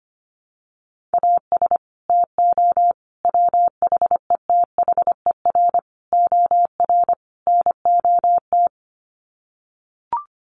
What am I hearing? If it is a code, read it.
Morse code, heard as a keyed tone, about 25 wpm, 700 Hz. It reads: AS TO WHETHER OR NOT
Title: Sets of 5 Words Encore 25wpm